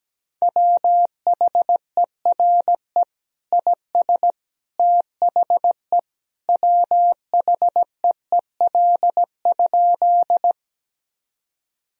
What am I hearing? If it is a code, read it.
Morse code, heard as a keyed tone, about 17 wpm, 700 Hz. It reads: WHERE IS THE WHEEL?
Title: Sentences from Top 1000 Words - Review Part 3 - 17wpm